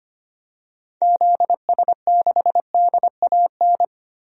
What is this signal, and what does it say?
Morse code, heard as a keyed tone, about 25 wpm, 700 Hz. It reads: ZS6DAN
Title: DXCC-100 Call Signs - 6 Characters - Repeated 3x - 25wpm